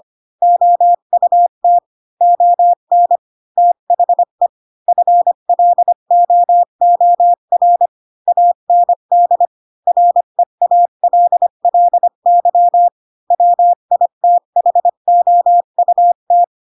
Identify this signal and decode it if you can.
Morse code, heard as a keyed tone, about 25 wpm, 700 Hz. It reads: OUT ON THE FLOOR AND REALLY WITHOUT
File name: Pinocchio / Pinocchio ch7 20 25 ik1hge